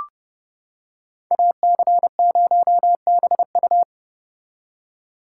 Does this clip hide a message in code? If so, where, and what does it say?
Yes: Morse code from 1.3 to 3.8 s, heard as a keyed tone, about 29 wpm, 700 Hz. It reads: AC0BU